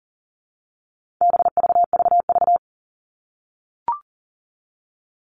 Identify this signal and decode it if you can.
Morse code, heard as a keyed tone, about 40 wpm, 700 Hz. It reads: BVVV
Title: All Cut Numbers - 4-Digit 40wpm